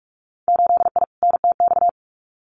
Morse code, heard as a keyed tone, about 46 wpm, 700 Hz. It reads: 8I NTX
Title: Winter Field Day Exchanges Encore 45wpm